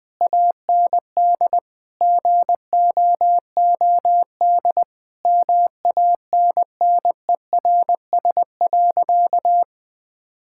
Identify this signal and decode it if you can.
Morse code, heard as a keyed tone, about 20 wpm, 700 Hz. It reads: AND GOOD MANNERS.
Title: Difficult Sentences with Hints - 20wpm